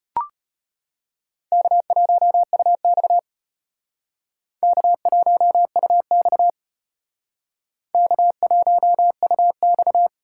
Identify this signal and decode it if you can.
Morse code, heard as a keyed tone, about 32 wpm, 700 Hz. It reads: K1UX K1UX K1UX